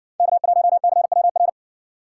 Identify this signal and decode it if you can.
Morse code, heard as a keyed tone, about 60 wpm, 700 Hz. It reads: K1CWR